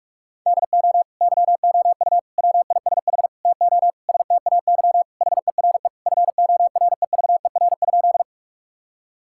Morse code, heard as a keyed tone, about 41 wpm, 700 Hz. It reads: DO YOU WISH TO STAY HERE FOREVER?
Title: Sentences from Top 1600 Words 45wpm